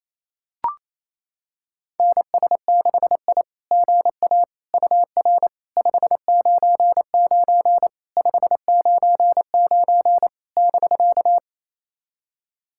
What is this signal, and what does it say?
Morse code, heard as a keyed tone, about 28 wpm, 700 Hz. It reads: NS6I GA UR 599 599 <BK>
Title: SOTA QSO Practice 28wpm